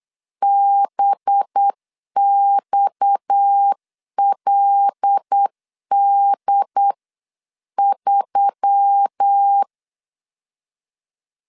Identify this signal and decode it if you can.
Morse code, heard as a keyed tone, about 8 wpm, 800 Hz. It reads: BXLD3
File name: CW / DET / 1 9 1500C S A 8wpm 5637